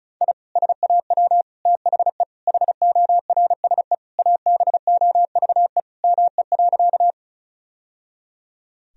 Morse code, heard as a keyed tone, about 36 wpm, 700 Hz. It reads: I SAW THE HORSE ABOVE ME.
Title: Sentences from Top 1000 Words - Review Part 2 - 35wpm